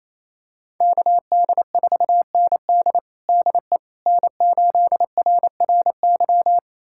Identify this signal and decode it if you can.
Morse code, heard as a keyed tone, about 27 wpm, 700 Hz. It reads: KD4ND DE N8RRY
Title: USA Callsign DE Callsign - 5 Characters - 28wpm